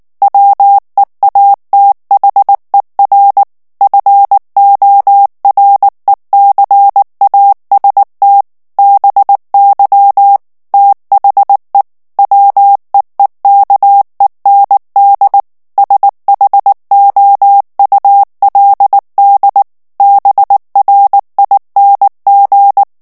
Morse code, heard as a keyed tone, about 19 wpm, 800 Hz. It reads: WEATHER FORECAST BY THE WEEKEND SHOULD BRING
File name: morse mp3 / ANGLAIS texte clair 2 vitesse 18 mots minute